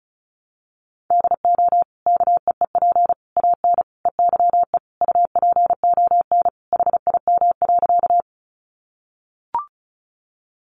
Morse code, heard as a keyed tone, about 34 wpm, 700 Hz. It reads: DO KEEP AN EYE UPON HIM.
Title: Sentences from Top 1100 Words 35wpm